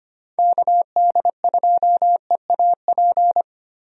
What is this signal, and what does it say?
Morse code, heard as a keyed tone, about 25 wpm, 700 Hz. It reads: KD2EAP
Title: USA Call Signs - 6 Characters - Repeated 2x - 25wpm